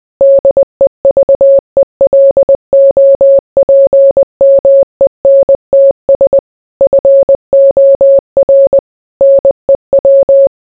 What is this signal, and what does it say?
Morse code, heard as a keyed tone, about 20 wpm, 550 Hz. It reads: DEVELOPMENTS FOR NEW